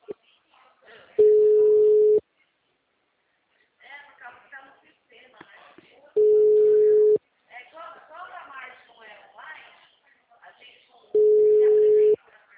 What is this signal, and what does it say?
Ringback tone (Europe, 425 Hz), with speech.
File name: out- -2008- - - .62855